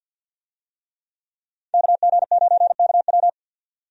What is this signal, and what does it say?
Morse code, heard as a keyed tone, about 49 wpm, 700 Hz. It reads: KG9KW
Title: USA Call Signs - 5 Characters - 50wpm